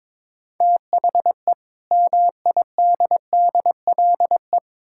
Morse code, heard as a keyed tone, about 22 wpm, 700 Hz. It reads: THE MIDDLE
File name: Build A Phrase - 2-Word Phrases - 22wpm